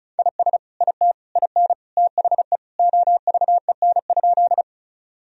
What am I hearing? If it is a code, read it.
Morse code, heard as a keyed tone, about 36 wpm, 700 Hz. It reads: IS IT IN THE OVEN?
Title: Sentences from Top 1000 Words - Review Part 4 - 35wpm